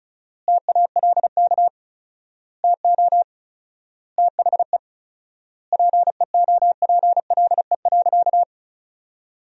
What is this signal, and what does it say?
Morse code, heard as a keyed tone, about 36 wpm, 700 Hz. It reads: TALK TO THE PEOPLE.
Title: 4x Word-Spacing - Sentences from Top 400 Words - 35wpm